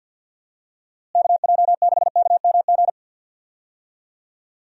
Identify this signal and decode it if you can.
Morse code, heard as a keyed tone, about 50 wpm, 700 Hz. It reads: KJ6KMG
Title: USA Call Signs - 6 Characters - Repeated 2x - 50wpm